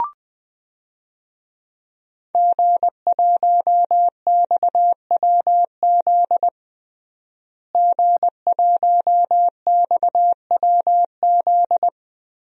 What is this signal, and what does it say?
Morse code, heard as a keyed tone, about 21 wpm, 700 Hz. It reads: G1XWZ G1XWZ
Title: DXCC-100 Call Signs - 5 Characters - Repeated 3x - 20wpm